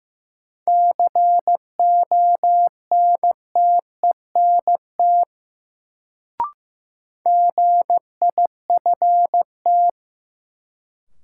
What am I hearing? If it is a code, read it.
Morse code, heard as a keyed tone, about 15 wpm, 700 Hz. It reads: CONTENT GIFT
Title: Sentences from Top 900 Words - New Word Drill - 15wpm